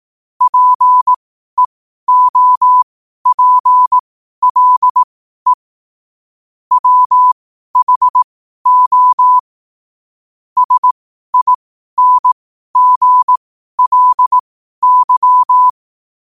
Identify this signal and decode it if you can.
Morse code, heard as a keyed tone, about 18 wpm, spaced as for 12 wpm, 1000 Hz. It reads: PEOPLE WHO SINGLY
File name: qotdfiles / 12wpm